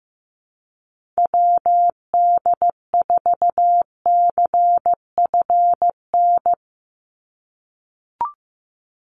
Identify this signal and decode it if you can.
Morse code, heard as a keyed tone, about 15 wpm, 700 Hz. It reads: WD4CFN
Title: NA SOTA Chaser Callsigns 15wpm